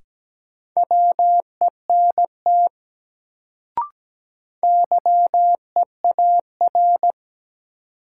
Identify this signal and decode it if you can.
Morse code, heard as a keyed tone, about 17 wpm, 700 Hz. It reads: WENT YEAR
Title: Sentences from Top 200 Words - New Word Drill - 17wpm